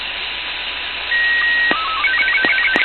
SSTV (slow-scan television), Robot 36.
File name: co8tw / Arissat1 28oct11 1727